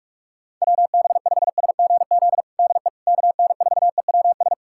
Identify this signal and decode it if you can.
Morse code, heard as a keyed tone, about 44 wpm, 700 Hz. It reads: WB5SGZ DE KN4EWS
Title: USA Callsign DE Callsign 45wpm